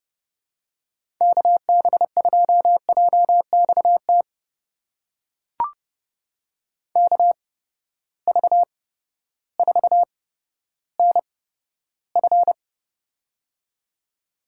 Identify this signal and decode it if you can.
Morse code, heard as a keyed tone, about 31 wpm, spaced as for 14 wpm, 700 Hz. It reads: KB2JXT KV4NF
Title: USA Call Signs - 8x Character Spacing - 30wpm